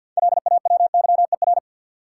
Morse code, heard as a keyed tone, about 50 wpm, 700 Hz. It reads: LAWYER
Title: Six Letter Words - 50wpm